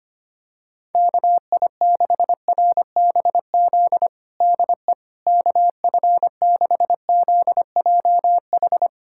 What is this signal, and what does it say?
Morse code, heard as a keyed tone, about 25 wpm, 700 Hz. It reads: KI6RBZ DE KF6ZJH
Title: USA Callsign DE Callsign - 6 Characters - 25wpm